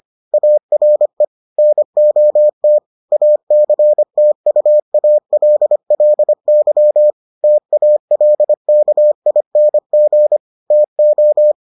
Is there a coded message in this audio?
Yes: Morse code from 0.3 to 11.5 s, heard as a keyed tone, about 25 wpm, 600 Hz. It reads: ARE NOT ACTUALLY TALKING TO